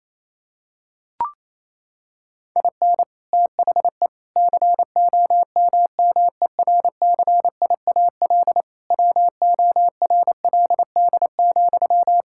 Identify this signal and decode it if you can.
Morse code, heard as a keyed tone, about 27 wpm, 700 Hz. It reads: IN THE COMMERCIAL WORLD,
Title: Sentences from Top 1000 Words - Review Part 5 - 28wpm